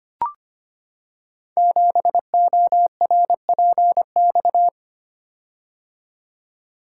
Morse code, heard as a keyed tone, about 25 wpm, 700 Hz. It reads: 7ORPX
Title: Code Groups - 5 Letter:Numbers - 25wpm